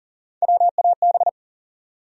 Morse code, heard as a keyed tone, about 40 wpm, 700 Hz. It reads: WAB
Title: USA Call Sign Postfixes - 3 Letters 40wpm